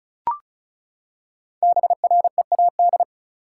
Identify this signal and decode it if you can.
Morse code, heard as a keyed tone, about 34 wpm, 700 Hz. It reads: BREAD
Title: Five Letter Words - 35wpm